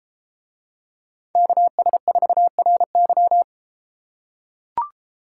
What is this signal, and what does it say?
Morse code, heard as a keyed tone, about 32 wpm, 700 Hz. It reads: KS4RY